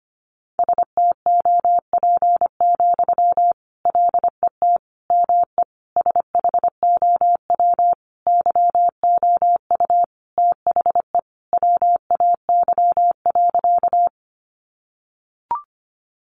Morse code, heard as a keyed tone, about 25 wpm, 700 Hz. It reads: STOP, LET ME SHOW YOU THE WAY.
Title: Sentences from Top 300 Words 25wpm